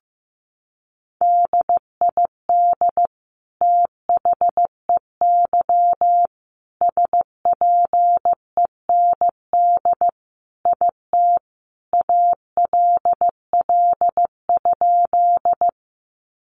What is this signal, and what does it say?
Morse code, heard as a keyed tone, about 15 wpm, 700 Hz. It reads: DID THEY SPEND IT ALL?
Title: Sentences from Top 800 Words 15wpm